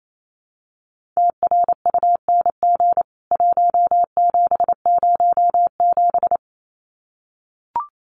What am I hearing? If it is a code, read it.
Morse code, heard as a keyed tone, about 28 wpm, 700 Hz. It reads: TRUNG 1707
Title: CWT Exchange 28wpm